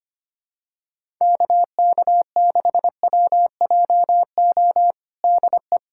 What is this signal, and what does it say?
Morse code, heard as a keyed tone, about 25 wpm, 700 Hz. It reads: KK6WJO DE
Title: USA Callsign DE Callsign 25wpm